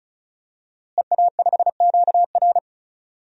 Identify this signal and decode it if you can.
Morse code, heard as a keyed tone, about 36 wpm, 700 Hz. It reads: EA5QR
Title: DXCC-100 Call Signs - 5 Characters - Repeated 2x - 35wpm